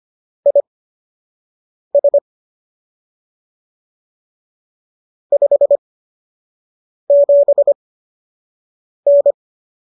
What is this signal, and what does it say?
Morse code, heard as a keyed tone, about 25 wpm, spaced as for 6 wpm, 570 Hz. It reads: IS 57N